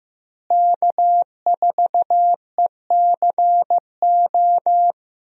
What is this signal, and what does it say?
Morse code, heard as a keyed tone, about 16 wpm, 700 Hz. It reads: K4ECO
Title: USA Call Signs - 5 Characters - Repeated 3x - 15wpm